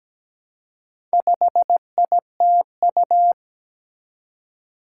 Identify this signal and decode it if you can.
Morse code, heard as a keyed tone, about 17 wpm, 700 Hz. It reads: 5ITU
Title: Four Challenging Letter-Number Combinations 17wpm